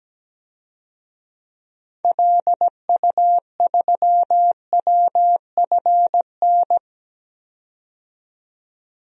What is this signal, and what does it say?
Morse code, heard as a keyed tone, about 17 wpm, 700 Hz. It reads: LU3WFN
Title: DXCC-100 Call Signs - 6 Characters - 17wpm